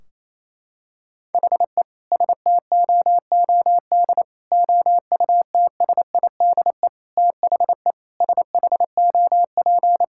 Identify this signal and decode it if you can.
Morse code, heard as a keyed tone, about 28 wpm, 700 Hz. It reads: HE STOOD OUTSIDE THE SHOP
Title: Sentences from Top 1600 Words 28wpm